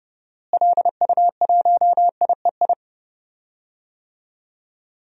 Morse code, heard as a keyed tone, about 29 wpm, 700 Hz. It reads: LU1IEI
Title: DXCC-100 Call Signs - 6 Characters - Repeated 2x - 30wpm